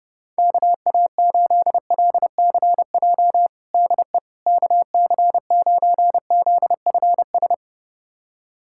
Morse code, heard as a keyed tone, about 30 wpm, 700 Hz. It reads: KA8LCJ DE KC9ZFS
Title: USA Callsign DE Callsign - 6 Characters - 30wpm